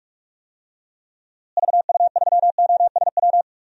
Morse code, heard as a keyed tone, about 45 wpm, 700 Hz. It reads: VU3OSW
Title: DXCC-100 Call Signs - 6 Characters - Repeated 2x - 45wpm